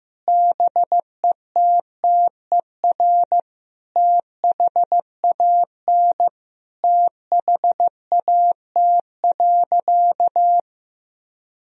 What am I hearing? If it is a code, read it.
Morse code, heard as a keyed tone, about 15 wpm, 700 Hz. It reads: BETTER THAN THAT.
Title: Sentences from Top 1000 Words - Review Part 4 - 15wpm